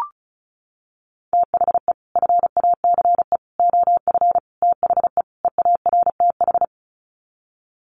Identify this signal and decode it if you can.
Morse code, heard as a keyed tone, about 34 wpm, 700 Hz. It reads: THE FACE OF THE EARTH
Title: Sets of 5 Words Encore 35wpm